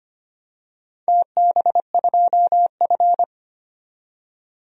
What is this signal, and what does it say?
Morse code, heard as a keyed tone, about 25 wpm, 700 Hz. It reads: TB2F
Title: DXCC-100 Call Signs - 4 Characters - 25wpm